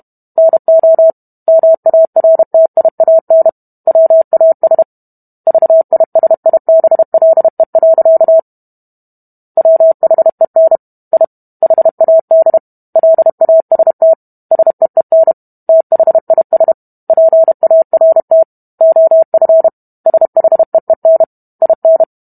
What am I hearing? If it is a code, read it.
Morse code, heard as a keyed tone, about 32 wpm, 650 Hz. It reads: NO MARTIAN WAS VISIBLE. WHEN I HAD LAST SEEN THIS PART OF SHEEN IN